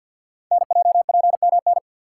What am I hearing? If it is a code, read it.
Morse code, heard as a keyed tone, about 51 wpm, 700 Hz. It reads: NJPMN
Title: Code Groups - 5 Letters - 50wpm